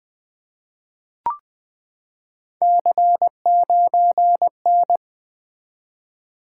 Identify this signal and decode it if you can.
Morse code, heard as a keyed tone, about 20 wpm, 700 Hz. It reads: C9N
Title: Three Letter-Number Combinations - 20wpm